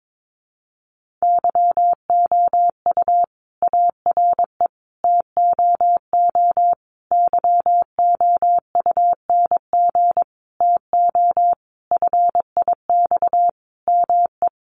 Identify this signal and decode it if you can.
Morse code, heard as a keyed tone, about 22 wpm, 700 Hz. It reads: YOU ARE TOO YOUNG TO FIX ME
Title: Sentences from Top 1400 Words 22wpm